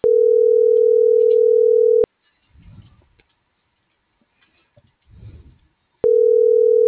Ringback tone (North America), 440 + 480 Hz.